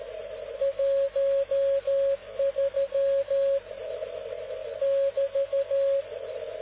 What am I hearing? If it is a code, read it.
Morse code, heard as a keyed tone, about 13 wpm, 550 Hz. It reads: 13 <BT>